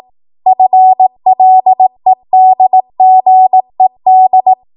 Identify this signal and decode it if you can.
Morse code, heard as a keyed tone, about 18 wpm, 750 Hz. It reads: FLEDGED
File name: Morse / 18WPM